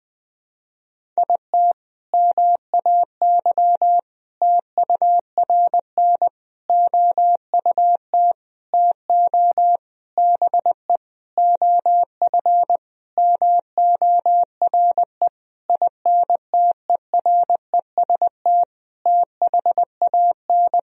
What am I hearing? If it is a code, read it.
Morse code, heard as a keyed tone, about 20 wpm, 700 Hz. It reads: IT MAY TURN OUT TO BE OF MORE INTEREST THAN
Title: Sentences from Top 400 Words and Abbreviations 20wpm